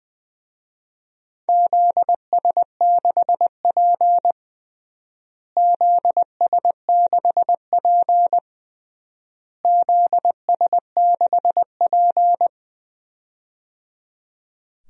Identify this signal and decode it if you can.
Morse code, heard as a keyed tone, about 20 wpm, 700 Hz. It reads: ZS6P ZS6P ZS6P
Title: DXCC-100 Call Signs - 4 Characters - Repeated 3x - 20wpm